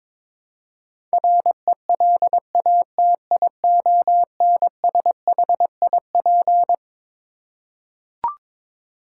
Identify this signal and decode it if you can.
Morse code, heard as a keyed tone, about 22 wpm, 700 Hz. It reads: RELATIONSHIP
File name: Twelve Letter Words - 22wpm